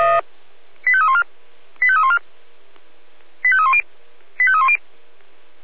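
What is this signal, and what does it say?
Five-tone selective call (ZVEI), heard as quick runs of five tones, tones of about 70 ms. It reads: 85314 85314 85310 85310
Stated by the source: ZVEI five-tone selective call 85894, received by SDR.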